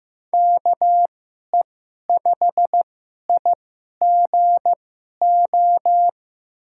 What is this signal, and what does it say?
Morse code, heard as a keyed tone, about 15 wpm, spaced as for 11 wpm, 700 Hz. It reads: KE5IGO
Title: USA Call Signs - 2x Character Spacing - 15wpm